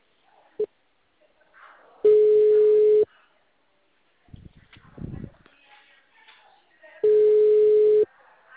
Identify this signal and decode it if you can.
Ringback tone (Europe, 425 Hz), with speech.